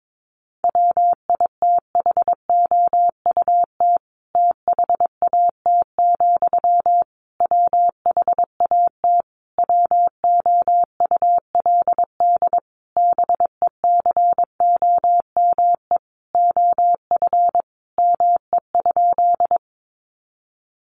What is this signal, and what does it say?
Morse code, heard as a keyed tone, about 22 wpm, 700 Hz. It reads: WITHOUT THAT, WHAT WOULD BECOME OF ME?
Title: Sentences from Top 600 Words 22wpm